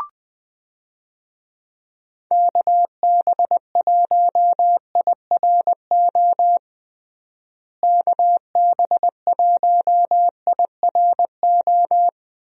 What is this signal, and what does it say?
Morse code, heard as a keyed tone, about 21 wpm, 700 Hz. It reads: KB1IRO KB1IRO
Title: USA Call Signs - 6 Characters - Repeated 3x - 20wpm